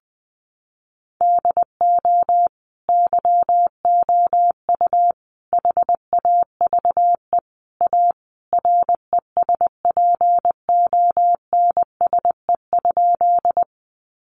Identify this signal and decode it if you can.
Morse code, heard as a keyed tone, about 20 wpm, 700 Hz. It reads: DO YOU HAVE A RESPONSE?
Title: Sentences from Top 1400 Words 20wpm